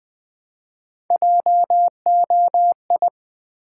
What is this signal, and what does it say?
Morse code, heard as a keyed tone, about 20 wpm, 700 Hz. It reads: JOI
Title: Build A Word - 1 Letter At A Time - 5-Letter Words - 20wpm